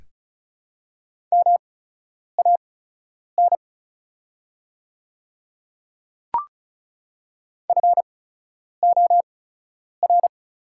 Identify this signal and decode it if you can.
Morse code, heard as a keyed tone, about 36 wpm, spaced as for 10 wpm, 690 Hz. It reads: MAN FOR
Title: 8x Character Spacing - Top 200 Words - 35wpm